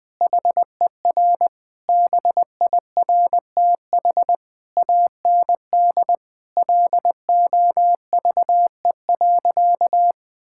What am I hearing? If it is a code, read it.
Morse code, heard as a keyed tone, about 20 wpm, 700 Hz. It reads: HER BIRTH AND LOVE.
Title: Sentences from Top 1000 Words - Review Part 5 - 20wpm